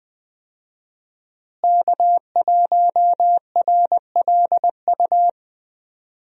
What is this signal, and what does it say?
Morse code, heard as a keyed tone, about 21 wpm, 700 Hz. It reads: K1RLU
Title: USA Call Signs - 5 Characters - Repeated 2x - 20wpm